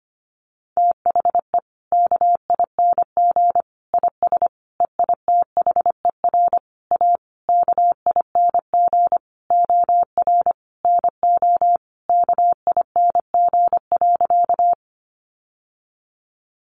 Morse code, heard as a keyed tone, about 25 wpm, 700 Hz. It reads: THE KING IS EITHER A KING OR NO KING.
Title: Sentences from Top 600 Words 25wpm